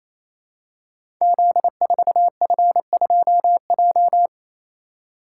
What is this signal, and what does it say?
Morse code, heard as a keyed tone, about 27 wpm, 700 Hz. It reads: Z4F2J
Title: Code Groups - 5 Letter:Numbers - 28wpm